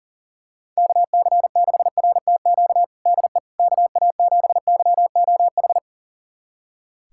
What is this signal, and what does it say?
Morse code, heard as a keyed tone, about 40 wpm, 700 Hz. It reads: KC6RTQ DE KA7YOH
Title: USA Callsign DE Callsign - 6 Characters - 40wpm